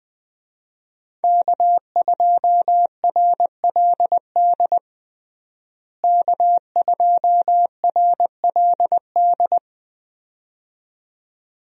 Morse code, heard as a keyed tone, about 20 wpm, 700 Hz. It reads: K2RLD K2RLD
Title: USA Call Signs - 5 Characters - Repeated 2x - 20wpm